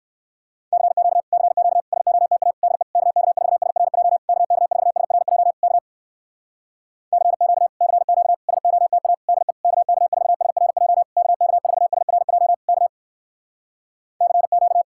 Morse code, heard as a keyed tone, about 67 wpm, 700 Hz. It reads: CQ CQ SOTA DE KK4SRJ KK4SRJ K CQ CQ SOTA DE KK4SRJ KK4SRJ K CQ